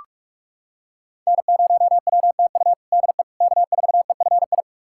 Morse code, heard as a keyed tone, about 41 wpm, 700 Hz. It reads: N0WTU DE K4EFI
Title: USA Callsign DE Callsign - 5 Characters - 45wpm